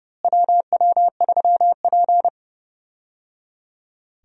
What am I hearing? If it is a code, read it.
Morse code, heard as a keyed tone, about 30 wpm, 700 Hz. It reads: WW3P
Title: USA Call Signs - 4 Characters - Repeated 3x - 30wpm